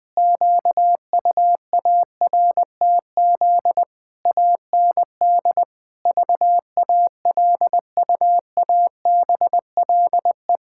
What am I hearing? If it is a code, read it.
Morse code, heard as a keyed tone, about 20 wpm, 700 Hz. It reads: QUARTZ AND VALUABLE
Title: Pangram Challenge v2 - 20wpm